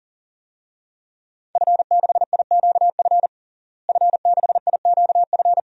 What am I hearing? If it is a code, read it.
Morse code, heard as a keyed tone, about 37 wpm, 700 Hz. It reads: F6IQF F6IQF
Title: DXCC-100 Call Signs - 5 Characters - Repeated 2x - 40wpm